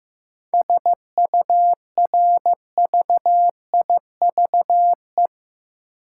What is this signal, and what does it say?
Morse code, heard as a keyed tone, about 15 wpm, 700 Hz. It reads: SURVIVE
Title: Sentences from Top 1300 Words - New Word Drill - 15wpm